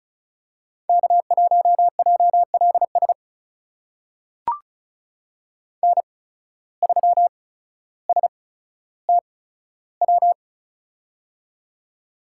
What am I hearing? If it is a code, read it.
Morse code, heard as a keyed tone, about 36 wpm, spaced as for 15 wpm, 700 Hz. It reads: K1JLS N3STW
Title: USA Call Signs - 5 Characters - 8x Character Spacing - 35wpm